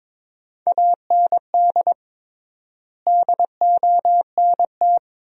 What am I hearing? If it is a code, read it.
Morse code, heard as a keyed tone, about 21 wpm, 700 Hz. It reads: AND DONT
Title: 3x Word-Spacing - Sentences from Top 400 Words - 22wpm